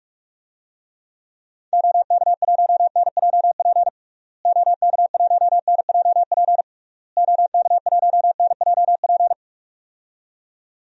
Morse code, heard as a keyed tone, about 46 wpm, 700 Hz. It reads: OK1NJP OK1NJP OK1NJP
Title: DXCC-100 Call Signs - 6 Characters - Repeated 3x - 45wpm